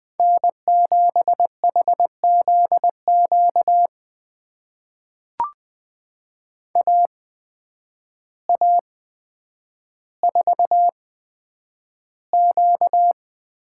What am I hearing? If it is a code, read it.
Morse code, heard as a keyed tone, about 20 wpm, spaced as for 9 wpm, 700 Hz. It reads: N7HZQ AA4Q
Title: USA Call Signs - 5 Characters - 8x Character Spacing - 20wpm